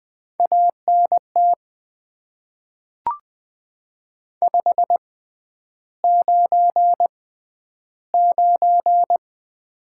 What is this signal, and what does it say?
Morse code, heard as a keyed tone, about 20 wpm, spaced as for 10 wpm, 700 Hz. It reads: ANT 599
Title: Top 90 QSO Elements - 6x Character Spacing - 20wpm